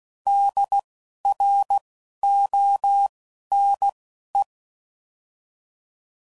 Morse code, heard as a keyed tone, about 16 wpm, spaced as for 11 wpm, 800 Hz. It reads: DRONE
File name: Prefixes / By First Letter / Ds